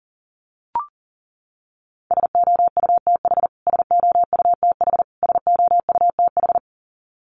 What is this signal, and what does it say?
Morse code, heard as a keyed tone, about 40 wpm, 700 Hz. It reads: SOUTH SOUTH SOUTH
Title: Top 500 Words Repeated 40wpm